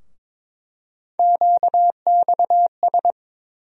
Morse code, heard as a keyed tone, about 22 wpm, 700 Hz. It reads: QXS
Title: USA Call Sign Postfixes - 3 Letters 22wpm